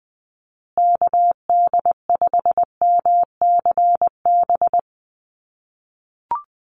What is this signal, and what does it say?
Morse code, heard as a keyed tone, about 20 wpm, 700 Hz. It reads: KD5MCB
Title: USA Call Signs Repeated 3x 20wpm